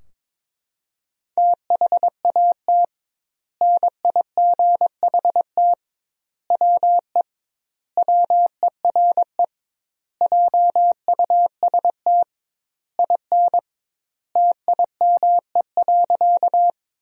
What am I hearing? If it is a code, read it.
Morse code, heard as a keyed tone, about 21 wpm, 700 Hz. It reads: THAT NIGHT WE WERE JUST IN TIME.
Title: 2x Word-Spacing - Sentences from Top 300 Words - 22wpm